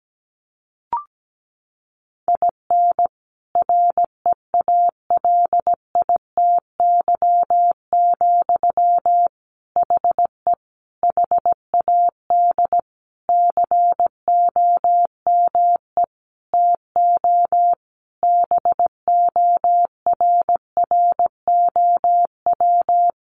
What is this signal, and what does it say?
Morse code, heard as a keyed tone, about 17 wpm, 700 Hz. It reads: IN REALITY, HE HAD COME TO BORROW
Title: Sentences from Top 1500 Words 17wpm